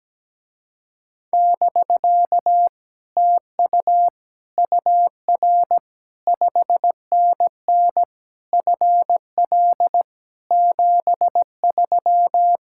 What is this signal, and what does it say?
Morse code, heard as a keyed tone, about 17 wpm, 700 Hz. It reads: <BK> TU UR 5NN FL 73
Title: Speed-Racing Common QSO Phrases - 17wpm